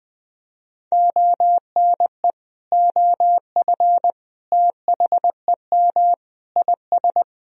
Morse code, heard as a keyed tone, about 20 wpm, 700 Hz. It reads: ONE OF THEM IS
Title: Sentences from Top 1000 Words - Review Part 4 - 20wpm